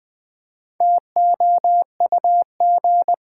Morse code, heard as a keyed tone, about 20 wpm, 700 Hz. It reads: TOUG
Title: Build A Word - 1 Letter At A Time - 5-Letter Words - 20wpm